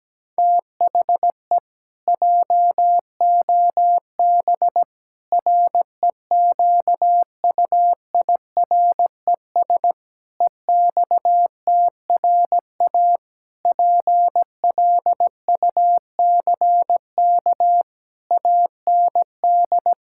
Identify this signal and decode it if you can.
Morse code, heard as a keyed tone, about 17 wpm, 700 Hz. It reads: THE JOB REQUIRES EXTRA PLUCK AND
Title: Pangram Challenge v2 - 17wpm